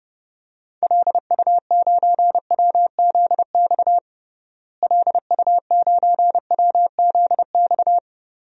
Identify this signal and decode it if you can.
Morse code, heard as a keyed tone, about 30 wpm, 700 Hz. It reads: LU9WZX LU9WZX
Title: DXCC-100 Call Signs - 6 Characters - Repeated 3x - 30wpm